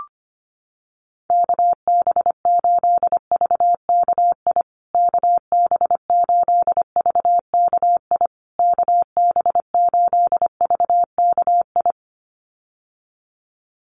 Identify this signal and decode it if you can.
Morse code, heard as a keyed tone, about 25 wpm, 700 Hz. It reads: KB8VKI KB8VKI KB8VKI
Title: USA Call Signs Repeated 3x 25wpm